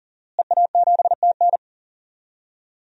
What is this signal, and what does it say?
Morse code, heard as a keyed tone, about 39 wpm, 700 Hz. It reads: EA7TN
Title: DXCC-100 Call Signs - 5 Characters - Repeated 2x - 40wpm